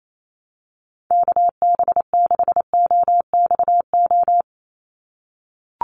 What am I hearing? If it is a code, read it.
Morse code, heard as a keyed tone, about 28 wpm, 700 Hz. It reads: KB6OXO
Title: USA Call Signs 28wpm